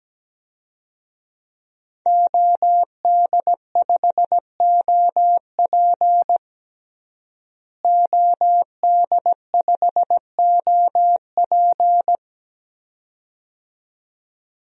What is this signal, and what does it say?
Morse code, heard as a keyed tone, about 17 wpm, 700 Hz. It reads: OD5OP OD5OP
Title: DXCC-100 Call Signs - 5 Characters - Repeated 2x - 17wpm